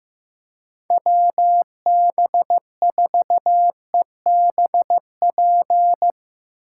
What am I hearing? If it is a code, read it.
Morse code, heard as a keyed tone, about 15 wpm, 700 Hz. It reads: WB4EBP
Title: USA Call Signs - 2x Character Spacing - 15wpm